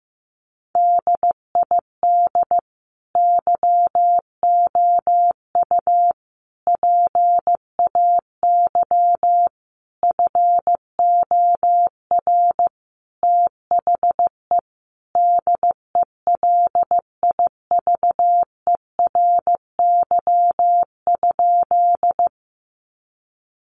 Morse code, heard as a keyed tone, about 15 wpm, 700 Hz. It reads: DID YOU PAY FOR THE DELIVERY?
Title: Sentences from Top 1500 Words 15wpm